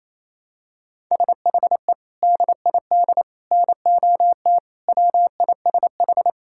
Morse code, heard as a keyed tone, about 27 wpm, 700 Hz. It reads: SHE DID NOT WISH
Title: Sentences from Top 1600 Words 28wpm